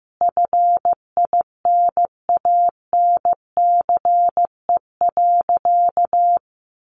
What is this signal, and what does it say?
Morse code, heard as a keyed tone, about 15 wpm, 700 Hz. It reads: FINANCE.
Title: Sentences from Top 1500 Words 15wpm